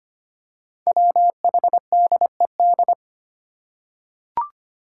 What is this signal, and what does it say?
Morse code, heard as a keyed tone, about 25 wpm, 700 Hz. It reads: WHDED
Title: Five Challenging Letter-Number Combinations 25wpm